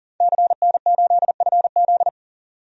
Morse code, heard as a keyed tone, about 38 wpm, 700 Hz. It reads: CN8FZ
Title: DXCC-100 Call Signs - 5 Characters - 40wpm